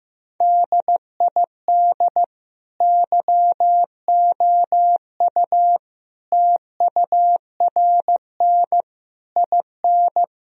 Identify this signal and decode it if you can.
Morse code, heard as a keyed tone, about 15 wpm, 700 Hz. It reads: DID YOU TURN IN
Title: Sentences from Top 1000 Words - Review Part 2 - 15wpm